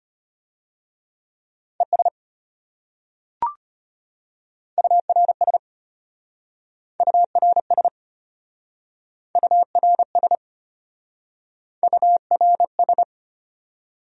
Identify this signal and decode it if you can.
Morse code, heard as a keyed tone, about 33 wpm, 700 Hz. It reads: ES URS URS URS URS